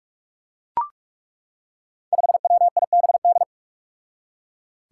Morse code, heard as a keyed tone, about 41 wpm, 700 Hz. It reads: 5WIBD
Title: Five Challenging Letter-Number Combinations 45wpm